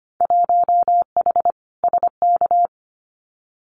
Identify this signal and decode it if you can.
Morse code, heard as a keyed tone, about 25 wpm, 700 Hz. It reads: 1H SK
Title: Winter Field Day Exchanges 25wpm